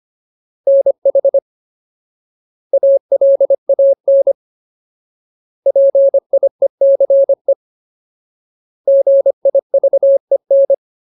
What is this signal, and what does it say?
Morse code, heard as a keyed tone, about 25 wpm, 550 Hz. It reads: NH ALAN PIECE GIVEN
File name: Ultimate Word List / 25WPM Ultimate